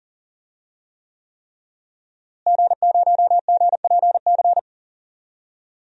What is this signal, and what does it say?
Morse code, heard as a keyed tone, about 40 wpm, 700 Hz. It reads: G0GPC
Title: DXCC-100 Call Signs - 5 Characters - Repeated 3x - 40wpm